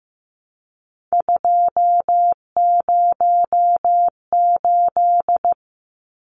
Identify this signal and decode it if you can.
Morse code, heard as a keyed tone, about 15 wpm, 700 Hz. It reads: 208
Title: Numbers - 3-Digit 15wpm